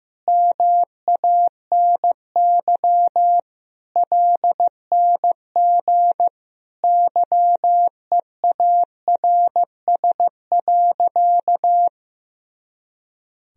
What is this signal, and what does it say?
Morse code, heard as a keyed tone, about 15 wpm, 700 Hz. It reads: MANY LNG YEARS.
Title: Sentences from Top 200 Words and Abbreviations 15wpm